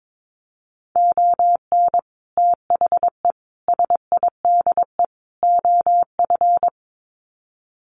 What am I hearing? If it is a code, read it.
Morse code, heard as a keyed tone, about 22 wpm, 700 Hz. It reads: ON THE SIDE OF
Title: Sets of 4 Words 22wpm